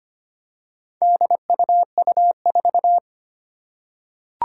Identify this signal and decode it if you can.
Morse code, heard as a keyed tone, about 25 wpm, 700 Hz. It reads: DUU4
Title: Four Challenging Letter-Number Combinations 25wpm